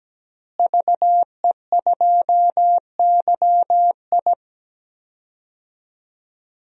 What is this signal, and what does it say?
Morse code, heard as a keyed tone, about 17 wpm, 700 Hz. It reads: VE2YI
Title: DXCC-100 Call Signs - 5 Characters - Repeated 3x - 17wpm